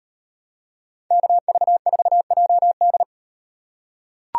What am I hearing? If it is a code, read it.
Morse code, heard as a keyed tone, about 38 wpm, 700 Hz. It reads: KV4JD